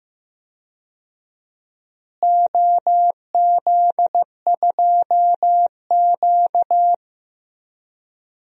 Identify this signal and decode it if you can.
Morse code, heard as a keyed tone, about 15 wpm, 700 Hz. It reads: OZ2Q
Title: DXCC-100 Call Signs - 4 Characters - Repeated 2x - 15wpm